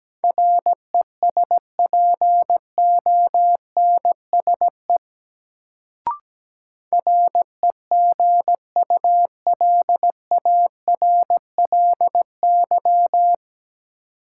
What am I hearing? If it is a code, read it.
Morse code, heard as a keyed tone, about 17 wpm, 700 Hz. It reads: RESPONSE REGULARLY
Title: Sentences from Top 1400 Words - New Word Drill Alt - 17wpm